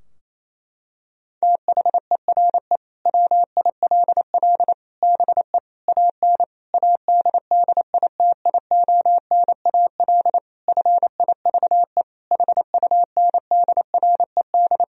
Morse code, heard as a keyed tone, about 28 wpm, 700 Hz. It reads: THERE WILL BE AN ADDITIONAL FIVE HUNDRED
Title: Sentences from Top 1000 Words - Review Part 4 - 28wpm